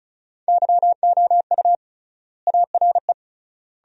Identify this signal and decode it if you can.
Morse code, heard as a keyed tone, about 34 wpm, 700 Hz. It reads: YOU ARE
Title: 3x Word-Spacing - Sentences from Top 400 Words - 35wpm